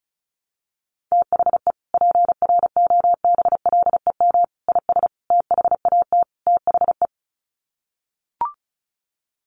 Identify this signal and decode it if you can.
Morse code, heard as a keyed tone, about 34 wpm, 700 Hz. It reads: THE PROBLEM IS THAT THE
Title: Sets of 5 Words Encore2 35wpm